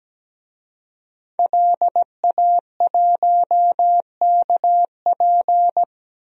Morse code, heard as a keyed tone, about 17 wpm, 700 Hz. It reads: LA1KP
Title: DXCC-100 Call Signs - 5 Characters - 17wpm